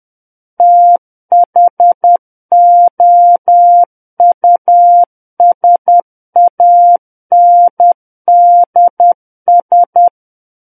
Morse code, heard as a keyed tone, about 10 wpm, 700 Hz. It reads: THOUSANDS
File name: Book One / 01-Chapter 1 - 10 WPM